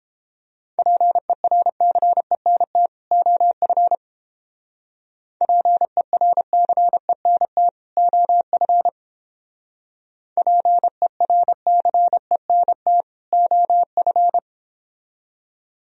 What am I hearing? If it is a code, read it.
Morse code, heard as a keyed tone, about 29 wpm, 700 Hz. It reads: PERCENT OF PERCENT OF PERCENT OF